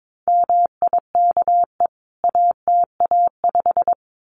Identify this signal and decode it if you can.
Morse code, heard as a keyed tone, about 22 wpm, 700 Hz. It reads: MIKE ATA5
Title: CWT Exchange 22wpm